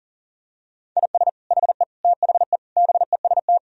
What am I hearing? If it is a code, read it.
Morse code, heard as a keyed tone, about 37 wpm, 700 Hz. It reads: IS HE THE BEST
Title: Sentences from Top 1000 Words - Review Part 4 - 40wpm